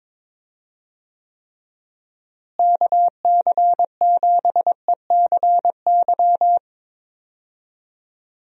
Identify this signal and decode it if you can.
Morse code, heard as a keyed tone, about 22 wpm, 700 Hz. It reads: KC7ECY
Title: USA Call Signs - 6 Characters - Repeated 3x - 22wpm